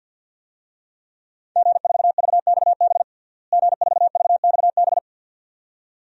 Morse code, heard as a keyed tone, about 49 wpm, 700 Hz. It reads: G4VXB G4VXB
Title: DXCC-100 Call Signs - 5 Characters - Repeated 2x - 50wpm